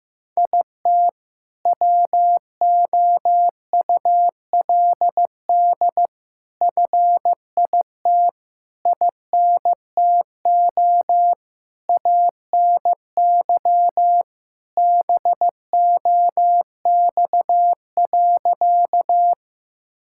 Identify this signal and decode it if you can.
Morse code, heard as a keyed tone, about 15 wpm, 700 Hz. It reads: IT WOULD FIT INTO ANY BOX.
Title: Sentences from Top 1000 Words - Review Part 5 - 15wpm